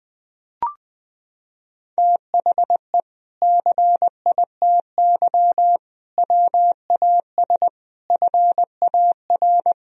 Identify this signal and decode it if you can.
Morse code, heard as a keyed tone, about 19 wpm, 700 Hz. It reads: THE CITY WAS FAR
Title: Sentences from Top 1000 Words - Review Part 1 - 20wpm